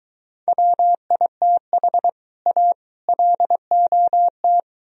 Morse code, heard as a keyed tone, about 23 wpm, 700 Hz. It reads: WITH A LOT